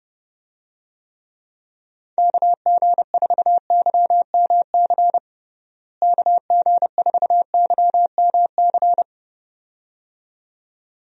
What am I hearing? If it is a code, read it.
Morse code, heard as a keyed tone, about 32 wpm, 700 Hz. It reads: KG4YMC KG4YMC
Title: USA Call Signs - 6 Characters - Repeated 2x - 30wpm